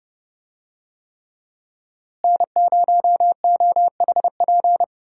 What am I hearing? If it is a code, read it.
Morse code, heard as a keyed tone, about 30 wpm, 700 Hz. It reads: N0OHP
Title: USA Call Signs - 5 Characters - Repeated 3x - 30wpm